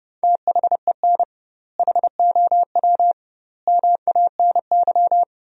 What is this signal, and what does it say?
Morse code, heard as a keyed tone, about 32 wpm, 700 Hz. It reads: THEN HOW MANY
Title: 2x Word-Spacing - Sentences from Top 100 Words - 30wpm